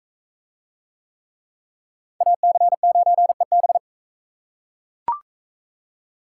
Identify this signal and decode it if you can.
Morse code, heard as a keyed tone, about 38 wpm, 700 Hz. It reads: AC9EB